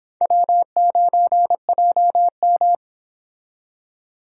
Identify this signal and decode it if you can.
Morse code, heard as a keyed tone, about 30 wpm, 700 Hz. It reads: W9JM